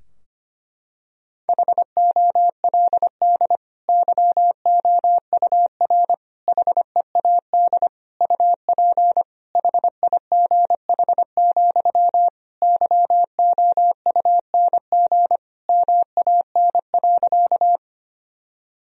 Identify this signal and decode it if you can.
Morse code, heard as a keyed tone, about 25 wpm, 700 Hz. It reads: HOLD YOUR HEAD UP HIGH, YOUNG MAN.
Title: Sentences from Top 1000 Words - Review Part 2 - 25wpm